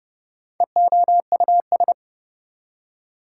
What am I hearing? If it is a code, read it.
Morse code, heard as a keyed tone, about 29 wpm, 700 Hz. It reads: EOUS
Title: Prefixes and Suffixes 30wpm